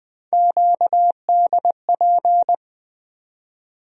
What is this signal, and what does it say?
Morse code, heard as a keyed tone, about 20 wpm, 700 Hz. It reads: QDP
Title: USA Call Sign Postfixes - 3 Letters 20wpm